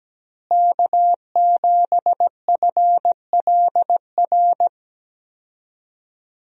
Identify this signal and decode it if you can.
Morse code, heard as a keyed tone, about 17 wpm, 700 Hz. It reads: K7FLR
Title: USA Call Signs - 5 Characters - Repeated 3x - 17wpm